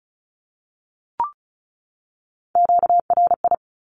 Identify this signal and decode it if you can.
Morse code, heard as a keyed tone, about 34 wpm, 700 Hz. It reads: QRI
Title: Q-Codes 35wpm